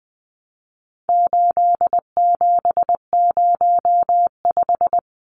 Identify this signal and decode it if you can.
Morse code, heard as a keyed tone, about 20 wpm, 700 Hz. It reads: 8705
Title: Numbers - 4-Digit 20wpm